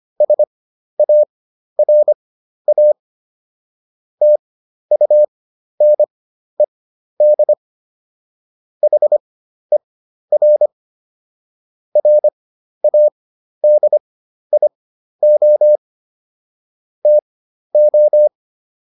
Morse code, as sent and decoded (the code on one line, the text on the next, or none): ... .- .-. .- / - ..- -. -.. / .... .-. / .-. .- -.. .. --- / - ---
SARA TUNED HER RADIO TO